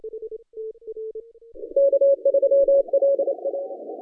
Morse code, heard as a keyed tone, about 28 wpm, 560 Hz. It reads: K3LA
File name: 2015 cq ww cw / 2015 cqwww k5tr - 0-2